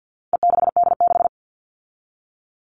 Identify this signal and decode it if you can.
Morse code, heard as a keyed tone, about 48 wpm, 700 Hz. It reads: E6D6
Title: All Cut Numbers - 4-Digit 50wpm